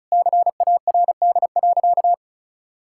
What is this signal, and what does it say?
Morse code, heard as a keyed tone, about 36 wpm, 700 Hz. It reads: CARD.
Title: Sentences from Top 1000 Words - Review Part 6 - 35wpm